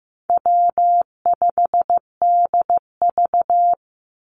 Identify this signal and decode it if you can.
Morse code, heard as a keyed tone, about 15 wpm, 700 Hz. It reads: W5DV
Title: Easy Call Signs - 15wpm